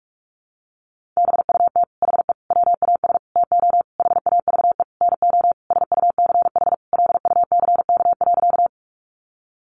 Morse code, heard as a keyed tone, about 44 wpm, 700 Hz. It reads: BUT HE WAS TO HAVE NO SUCH LUCK.
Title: Sentences from Top 1500 Words 45wpm